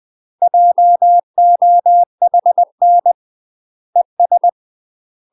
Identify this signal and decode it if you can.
Morse code, heard as a keyed tone, about 20 wpm, 700 Hz. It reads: JOHN ES
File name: UK MASTER QSOs TRAINING / 20WPM / UK MASTER QSOs 0035